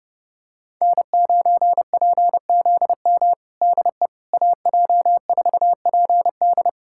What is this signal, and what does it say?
Morse code, heard as a keyed tone, about 30 wpm, 700 Hz. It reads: N9PZM DE AJ4PD
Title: USA Callsign DE Callsign - 5 Characters - 30wpm